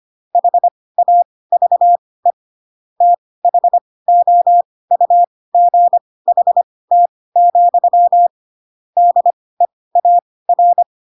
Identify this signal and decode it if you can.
Morse code, heard as a keyed tone, about 25 wpm, spaced as for 18 wpm, 700 Hz. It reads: HAVE THOUGHT, DEAR
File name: Pinocchio / Pinocchio ch7 12 25 ik1hge